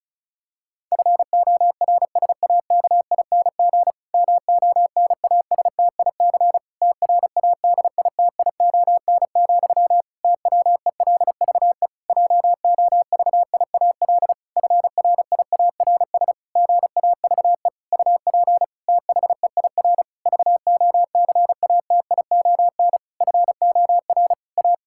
Morse code, heard as a keyed tone, about 34 wpm, 700 Hz. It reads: FORSAKING MONASTIC TRADITION, TWELVE JOVIAL FRIARS GAVE UP THEIR VOCATION FOR A
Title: Pangram Challenge v2 - 35wpm